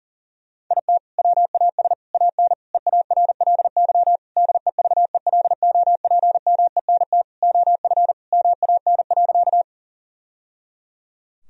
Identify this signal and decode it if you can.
Morse code, heard as a keyed tone, about 40 wpm, 700 Hz. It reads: IT WAS AN EARLY DEVELOPMENT OF MAN.
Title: Sentences from Top 1000 Words - Review Part 4 - 40wpm